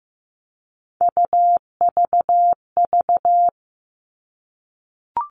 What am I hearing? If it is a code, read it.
Morse code, heard as a keyed tone, about 15 wpm, 700 Hz. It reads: UVV
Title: All Cut Numbers - 3-Digit 15wpm